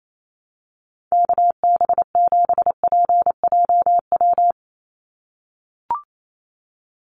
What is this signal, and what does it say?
Morse code, heard as a keyed tone, about 28 wpm, 700 Hz. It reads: KB7PJW
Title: USA Call Signs 28wpm